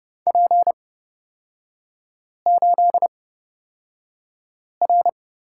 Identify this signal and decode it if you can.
Morse code, heard as a keyed tone, about 29 wpm, spaced as for 5 wpm, 700 Hz. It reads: P8R
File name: Lesson 37e - Letter x - Three Characters 30wpm